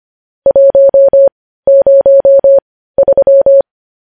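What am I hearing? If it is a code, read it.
Morse code, heard as a keyed tone, about 25 wpm, spaced as for 15 wpm, 550 Hz. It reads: 103
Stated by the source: Morse code news headlines at 15 WPM.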